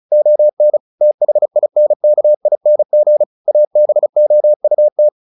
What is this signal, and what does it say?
Morse code, heard as a keyed tone, about 37 wpm, 600 Hz. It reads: ON THINKING ABOUT
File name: the old man and the sea 2 35wpm